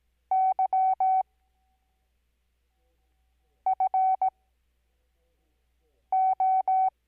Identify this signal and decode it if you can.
Morse code, heard as a keyed tone, about 18 wpm, spaced as for 4 wpm, 760 Hz. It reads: YFO
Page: Farnsworth Spacing: 15WPM Character speed Spaced at 5WPM.